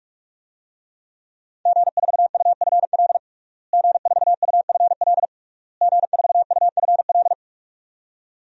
Morse code, heard as a keyed tone, about 43 wpm, 700 Hz. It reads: G4UFL G4UFL G4UFL
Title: DXCC-100 Call Signs - 5 Characters - Repeated 3x - 45wpm